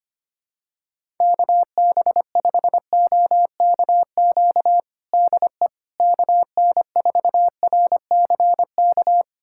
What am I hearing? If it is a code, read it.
Morse code, heard as a keyed tone, about 25 wpm, 700 Hz. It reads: KB5OKQ DE KN4RCK
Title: USA Callsign DE Callsign 25wpm